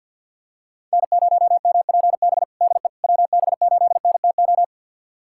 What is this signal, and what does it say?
Morse code, heard as a keyed tone, about 47 wpm, 690 Hz. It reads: N0MPB DE WB8NTO
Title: USA Callsign DE Callsign 50wpm